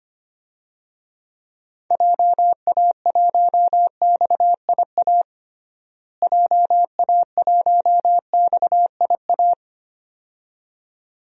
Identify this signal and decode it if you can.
Morse code, heard as a keyed tone, about 25 wpm, 700 Hz. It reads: JA1XIA JA1XIA
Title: DXCC-100 Call Signs - 6 Characters - Repeated 2x - 25wpm